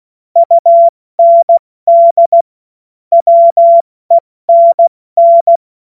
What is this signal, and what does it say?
Morse code, heard as a keyed tone, about 16 wpm, 680 Hz. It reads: UND WENN
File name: Morsekurs / Witze / Witz 06. Januar 2026 70 F80